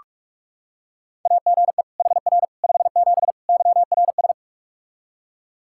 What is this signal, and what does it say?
Morse code, heard as a keyed tone, about 41 wpm, 700 Hz. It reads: AGE HR 57 YRS
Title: Practice QSO - Exchange 3 - More Info - 45wpm